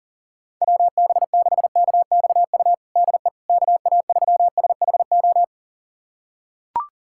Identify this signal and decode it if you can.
Morse code, heard as a keyed tone, about 40 wpm, 700 Hz. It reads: WB6KXU DE KA3SHO
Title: USA Callsign DE Callsign 40wpm